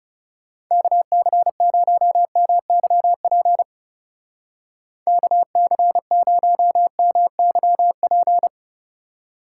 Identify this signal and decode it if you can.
Morse code, heard as a keyed tone, about 37 wpm, 700 Hz. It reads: KC0MYP KC0MYP
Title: Speed Racing - USA 2x3 Call Signs 30wpm